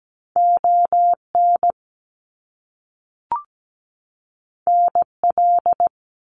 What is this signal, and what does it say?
Morse code, heard as a keyed tone, about 17 wpm, 700 Hz. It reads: ON NL
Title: Canadian Provinces and Territories 17wpm